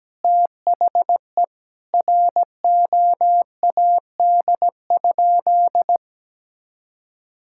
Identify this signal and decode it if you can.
Morse code, heard as a keyed tone, about 17 wpm, 700 Hz. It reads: THE ROAD?
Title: Sentences from Top 1000 Words - Review Part 5 - 17wpm